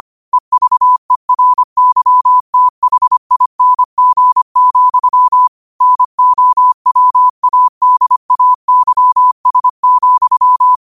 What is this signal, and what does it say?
Morse code, heard as a keyed tone, about 25 wpm, 1000 Hz. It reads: EVERYTHING, NOWADAYS,
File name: qotdfiles / 25wpm